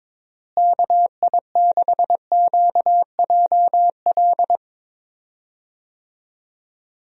Morse code, heard as a keyed tone, about 21 wpm, 700 Hz. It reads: KI6QJL
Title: USA Call Signs - 6 Characters - Repeated 2x - 22wpm